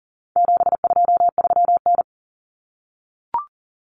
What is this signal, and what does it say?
Morse code, heard as a keyed tone, about 40 wpm, 700 Hz. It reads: 723N
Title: Common Cut Numbers - 4-Digit 40wpm